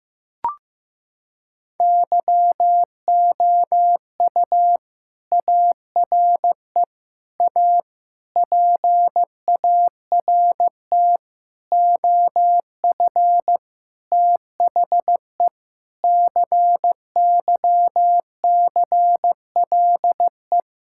Morse code, heard as a keyed tone, about 15 wpm, 700 Hz. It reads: YOU ARE A PART OF THE CYCLE
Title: Sentences from Top 1000 Words - Review Part 6 - 15wpm